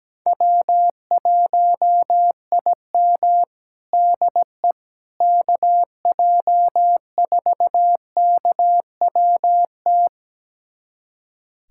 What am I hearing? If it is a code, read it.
Morse code, heard as a keyed tone, about 17 wpm, 700 Hz. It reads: W1IM DE KJ4KWT
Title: USA Callsign DE Callsign 17wpm